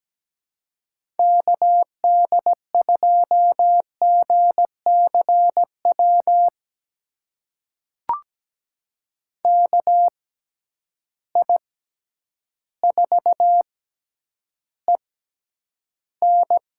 Morse code, heard as a keyed tone, about 17 wpm, spaced as for 9 wpm, 700 Hz. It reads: KD2GCW KI4EN
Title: USA Call Signs - 6 Characters - 6x Character Spacing - 17wpm